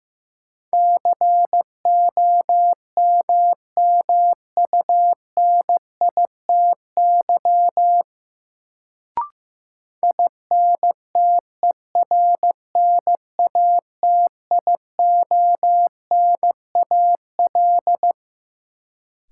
Morse code, heard as a keyed tone, about 15 wpm, 700 Hz. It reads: COMMUNITY INTERNATIONAL
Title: Sentences from Top 1000 Words - Review Part 4 - New Word Drill - 15wpm